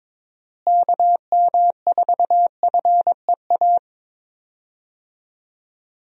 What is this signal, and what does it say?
Morse code, heard as a keyed tone, about 22 wpm, 700 Hz. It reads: KM4FEA
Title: USA Call Signs - 6 Characters - Repeated 3x - 22wpm